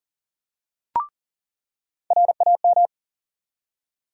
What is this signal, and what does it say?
Morse code, heard as a keyed tone, about 40 wpm, 700 Hz. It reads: RAM
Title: USA Call Sign Postfixes - 3 Letters 40wpm